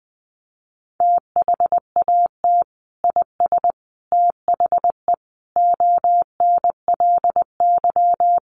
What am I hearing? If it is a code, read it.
Morse code, heard as a keyed tone, about 20 wpm, 700 Hz. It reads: THAT IS THE ONLY
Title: Sentences from Top 800 Words 20wpm